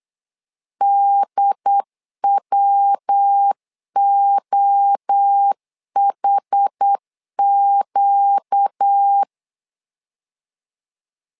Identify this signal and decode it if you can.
Morse code, heard as a keyed tone, about 8 wpm, 800 Hz. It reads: DWOHQ
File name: LEZIONI / 4 Q1S3 8wpm 3037